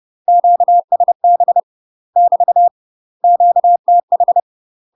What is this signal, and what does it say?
Morse code, heard as a keyed tone, about 30 wpm, 700 Hz. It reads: QSB <BT> QTH
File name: UK MASTER QSOs TRAINING / 30WPM / UK MASTER QSOs 0050